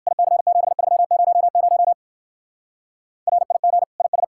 Morse code, heard as a keyed tone, about 56 wpm, 700 Hz. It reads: IC7300 RIG IS